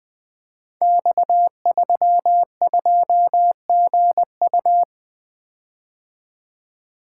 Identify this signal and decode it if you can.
Morse code, heard as a keyed tone, about 20 wpm, 700 Hz. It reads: X32GU
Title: Code Groups - 5 Letter:Numbers - 20wpm